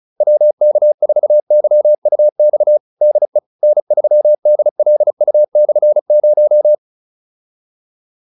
Morse code, heard as a keyed tone, about 34 wpm, 600 Hz. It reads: WK4YUX DE N3DLU/0